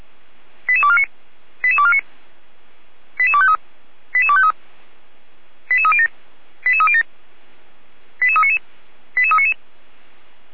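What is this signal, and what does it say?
Five-tone selective call (ZVEI), heard as quick runs of five tones, tones of about 70 ms. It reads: 88249 88249 88253 88253 88307 88307 88399 88399
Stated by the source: ZVEI five-tone selective call received by SDR, Feuerwehr Orlamünde fire brigade server.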